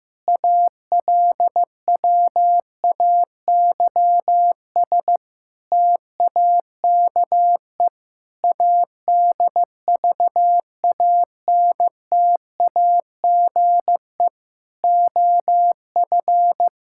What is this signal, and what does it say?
Morse code, heard as a keyed tone, about 15 wpm, 700 Hz. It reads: ALWAYS TAKE ADVANTAGE OF
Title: Sentences from Top 1000 Words - Review Part 6 - 15wpm